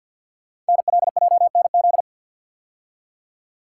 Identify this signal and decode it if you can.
Morse code, heard as a keyed tone, about 50 wpm, 700 Hz. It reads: NLJNZ
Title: Code Groups - 5 Letters - 50wpm